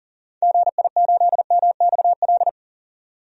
Morse code, heard as a keyed tone, about 37 wpm, 700 Hz. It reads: GI8MXL
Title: DXCC-100 Call Signs - 6 Characters - 40wpm